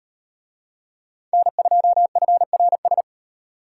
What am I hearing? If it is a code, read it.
Morse code, heard as a keyed tone, about 36 wpm, 700 Hz. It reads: N2FRS